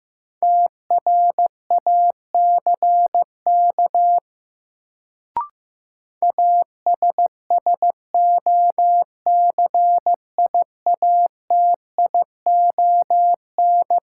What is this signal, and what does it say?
Morse code, heard as a keyed tone, about 15 wpm, 700 Hz. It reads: TRACK ASSOCIATION
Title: Sentences from Top 1100 Words - New Word Drill - 15wpm